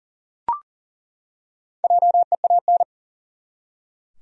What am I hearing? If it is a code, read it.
Morse code, heard as a keyed tone, about 43 wpm, 700 Hz. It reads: JEAN
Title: USA Common Names - Max 4 Letters - 40wpm